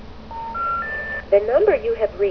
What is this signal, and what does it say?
Special information tone, with speech.